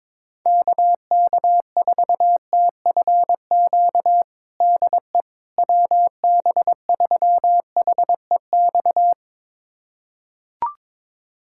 Morse code, heard as a keyed tone, about 22 wpm, 700 Hz. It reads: KK4TFQ DE WB3HEX
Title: USA Callsign DE Callsign - 6 Characters - 22wpm